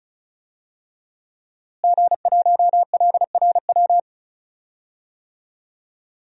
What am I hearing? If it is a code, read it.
Morse code, heard as a keyed tone, about 36 wpm, 700 Hz. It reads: G1LRW
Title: DXCC-100 Call Signs - 5 Characters - Repeated 3x - 35wpm